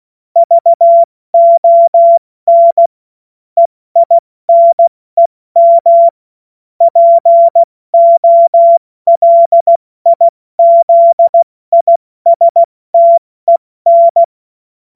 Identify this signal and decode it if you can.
Morse code, heard as a keyed tone, about 16 wpm, 680 Hz. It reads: VON EINEM POLIZISTEN
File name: Morsekurs / Witze / Witz 26. Oktober 2021, 70 F80